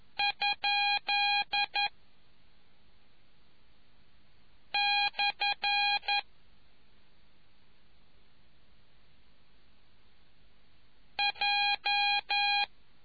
Morse code, heard as a keyed tone, about 11 wpm, 820 Hz. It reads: ? / J